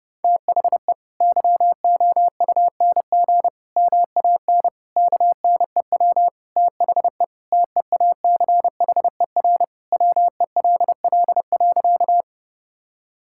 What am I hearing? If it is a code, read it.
Morse code, heard as a keyed tone, about 32 wpm, 700 Hz. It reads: THE YOUNG MAN KNEW THE TEACHER WELL.
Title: Sentences from Top 1000 Words - Review Part 6 - 30wpm